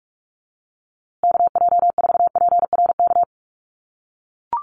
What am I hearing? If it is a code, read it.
Morse code, heard as a keyed tone, about 44 wpm, 700 Hz. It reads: KJ4PRK
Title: USA Call Signs Repeated 3x 45wpm